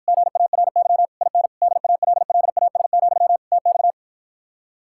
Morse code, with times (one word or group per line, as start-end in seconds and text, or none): 0.1-1.1 GARY
1.2-1.5 IN
1.6-3.4 DALLAS,
3.5-3.9 TX